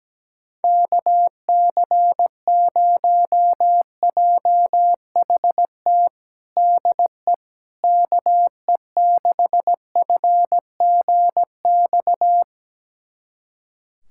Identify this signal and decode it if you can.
Morse code, heard as a keyed tone, about 17 wpm, 700 Hz. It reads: KC0JHT DE KE6FGX
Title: USA Callsign DE Callsign - 6 Characters - 17wpm